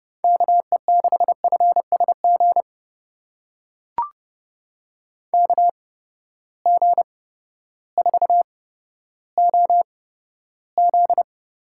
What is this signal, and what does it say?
Morse code, heard as a keyed tone, about 32 wpm, spaced as for 14 wpm, 700 Hz. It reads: KE6FSG KG4OZ
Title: USA Call Signs - 6 Characters - 8x Character Spacing - 30wpm